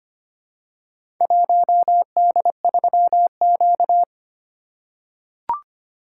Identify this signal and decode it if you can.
Morse code, heard as a keyed tone, about 25 wpm, 700 Hz. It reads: 1D3Q
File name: practice / 25wpm / Four Letter-Number Combinations - 25wpm